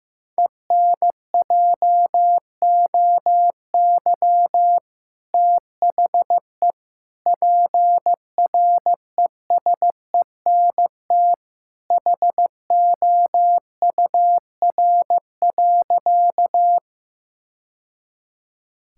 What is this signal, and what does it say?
Morse code, heard as a keyed tone, about 15 wpm, 690 Hz. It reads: ENJOY THE PRESENT HOUR.
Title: Sentences from Top 1000 Words - Review Part 5 - 15wpm